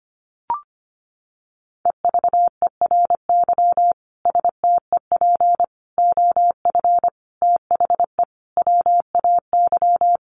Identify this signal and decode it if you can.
Morse code, heard as a keyed tone, about 25 wpm, 700 Hz. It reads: EVERY STEP OF THE WAY
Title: Sets of 5 Words Encore2 25wpm